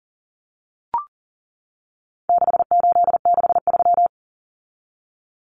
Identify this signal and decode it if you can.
Morse code, heard as a keyed tone, about 40 wpm, 700 Hz. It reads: 6863
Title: Common Cut Numbers - 4-Digit 40wpm